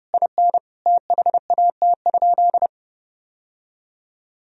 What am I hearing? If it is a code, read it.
Morse code, heard as a keyed tone, about 30 wpm, 700 Hz. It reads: IN THAT?
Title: Sentences from Top 1000 Words - Review Part 4 - 30wpm